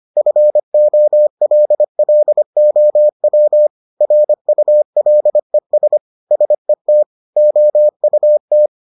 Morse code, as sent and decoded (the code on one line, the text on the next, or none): ..-. --- .-.. .-.. --- .-- / .-. ..- .-.. ... / ... - / --- ..- -
FOLLOW RULES SET OUT